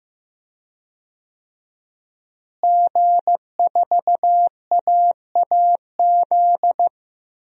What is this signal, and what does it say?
Morse code, heard as a keyed tone, about 15 wpm, 700 Hz. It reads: G4AAZ
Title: DXCC-100 Call Signs - 5 Characters - Repeated 2x - 15wpm